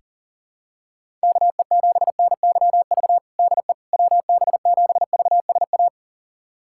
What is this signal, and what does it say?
Morse code, heard as a keyed tone, about 37 wpm, 700 Hz. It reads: KE7NYV DE WB7VSA
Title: USA Callsign DE Callsign 40wpm